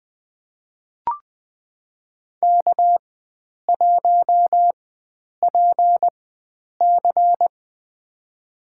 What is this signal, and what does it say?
Morse code, heard as a keyed tone, about 20 wpm, spaced as for 9 wpm, 700 Hz. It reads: K1PC
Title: USA Call Signs - 4 Characters - 4x Character Spacing - 20wpm